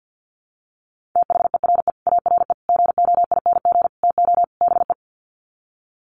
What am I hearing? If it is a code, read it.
Morse code, heard as a keyed tone, about 50 wpm, 700 Hz. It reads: THERE ARE GOING TO BE
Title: Sets of 5 Words 50wpm